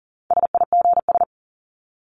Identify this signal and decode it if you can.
Morse code, heard as a keyed tone, about 40 wpm, 700 Hz. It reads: SIGS
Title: Abbreviations Repeated 3x - 40wpm